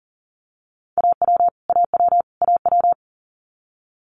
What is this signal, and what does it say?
Morse code, heard as a keyed tone, about 41 wpm, 700 Hz. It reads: AW AW AW
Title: Multi-letter Phonemes Repeated 3x 40wpm